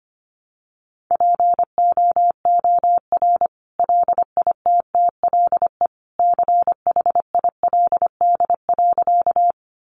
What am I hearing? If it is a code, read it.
Morse code, heard as a keyed tone, about 25 wpm, 700 Hz. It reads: POOR LITTLE CHILD.
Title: Sentences from Top 1000 Words 25wpm